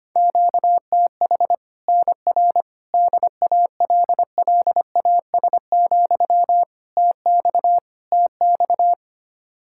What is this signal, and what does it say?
Morse code, heard as a keyed tone, about 25 wpm, 700 Hz. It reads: QTH NR DALLAS, TX TX
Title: Practice QSO - Exchange 1 - Beginning - 25wpm